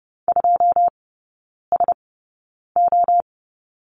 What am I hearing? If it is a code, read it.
Morse code, heard as a keyed tone, about 30 wpm, spaced as for 9 wpm, 700 Hz. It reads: 2SO
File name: 081 Lesson 14e - Number 2 - Three Characters 30wpm